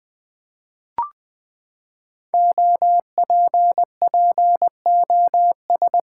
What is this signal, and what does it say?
Morse code, heard as a keyed tone, about 20 wpm, 700 Hz. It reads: OPPOS
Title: Build A Word - 5 Letters At A Time - 10-Letter Words - 20wpm